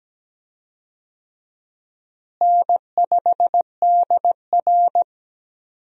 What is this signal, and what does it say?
Morse code, heard as a keyed tone, about 17 wpm, 700 Hz. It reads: N5DR
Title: USA Call Signs - 4 Characters - Repeated 2x - 17wpm